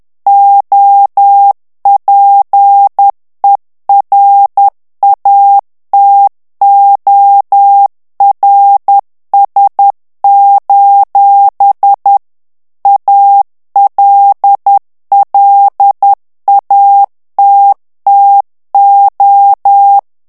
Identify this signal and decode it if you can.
Morse code, heard as a keyed tone, about 10 wpm, 800 Hz. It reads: OPERATORS: ALLATTO
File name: ITALIEN texte clair 2 vitesse 10 mots minute